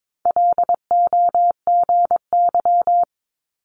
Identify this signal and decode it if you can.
Morse code, heard as a keyed tone, about 22 wpm, 700 Hz. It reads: LOGY
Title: Multi-letter Phonemes 22wpm